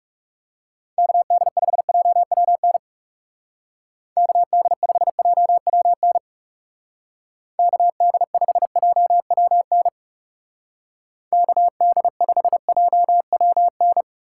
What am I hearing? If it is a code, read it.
Morse code, heard as a keyed tone, about 38 wpm, 690 Hz. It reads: KD5JWN KD5JWN KD5JWN KD5JWN